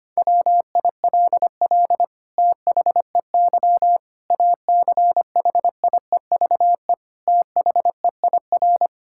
Morse code, heard as a keyed tone, about 25 wpm, 700 Hz. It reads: WILL THEY ACHIEVE THEIR
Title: Sentences from Top 1000 Words - Review Part 6 - 25wpm